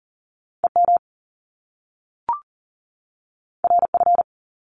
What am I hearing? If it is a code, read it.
Morse code, heard as a keyed tone, about 40 wpm, 700 Hz. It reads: EM RF
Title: Top Two Letter Combinations in Words 40wpm